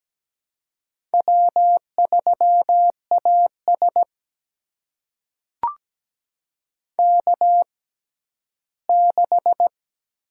Morse code, heard as a keyed tone, about 17 wpm, spaced as for 12 wpm, 700 Hz. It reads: W3AS K6
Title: USA Call Signs - 4 Characters - 6x Character Spacing - 17wpm